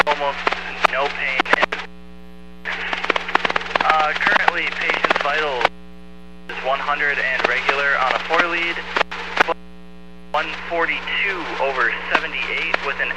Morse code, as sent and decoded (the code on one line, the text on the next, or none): none